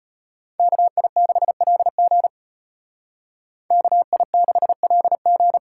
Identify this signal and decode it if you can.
Morse code, heard as a keyed tone, about 32 wpm, 700 Hz. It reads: KI6LG KI6LG